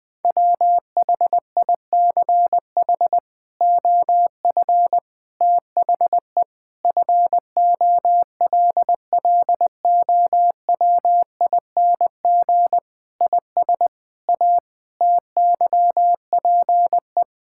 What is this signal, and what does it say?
Morse code, heard as a keyed tone, about 20 wpm, 700 Hz. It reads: WHICH OF THE FOLLOWING IS A TYPE
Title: Sentences with 8-Word Phrases - Spoken Beginning - 20wpm